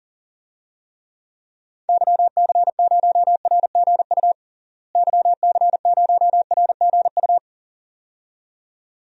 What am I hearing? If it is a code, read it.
Morse code, heard as a keyed tone, about 40 wpm, 700 Hz. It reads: YC0RGU YC0RGU
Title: DXCC-100 Call Signs - 6 Characters - Repeated 2x - 40wpm